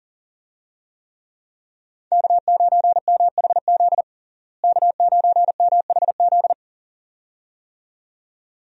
Morse code, heard as a keyed tone, about 40 wpm, 700 Hz. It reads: K9MHZ K9MHZ
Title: USA Call Signs - 5 Characters - Repeated 2x - 40wpm